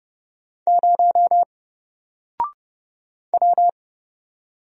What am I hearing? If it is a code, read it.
Morse code, heard as a keyed tone, about 30 wpm, 700 Hz. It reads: 0 W
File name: Single-Letter-Number-30wpm